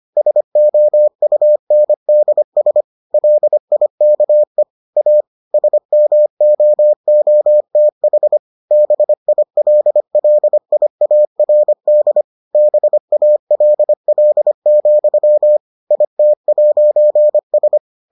Morse code, heard as a keyed tone, about 25 wpm, 600 Hz. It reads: SOUNDS LIKE A SMOOTH BILLIARD BALL, IT'S